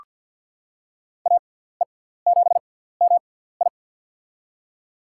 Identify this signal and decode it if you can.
Morse code, heard as a keyed tone, about 47 wpm, spaced as for 17 wpm, 700 Hz. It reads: AE7MI
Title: USA Call Signs - 5 Characters - 6x Character Spacing - 50wpm